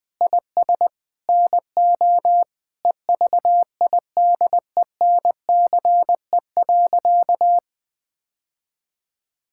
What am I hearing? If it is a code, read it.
Morse code, heard as a keyed tone, about 21 wpm, 700 Hz. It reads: IS NO EVIDENCE.
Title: Sentences with 4-Word Phrases - Spoken Beginning - 20wpm